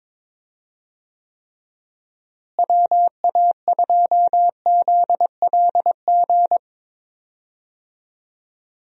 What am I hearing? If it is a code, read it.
Morse code, heard as a keyed tone, about 22 wpm, 700 Hz. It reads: WA2ZLG
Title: USA Call Signs - 6 Characters - Repeated 3x - 22wpm